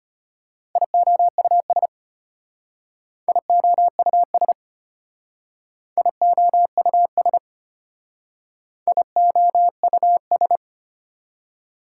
Morse code, heard as a keyed tone, about 32 wpm, 700 Hz. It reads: IOUS IOUS IOUS IOUS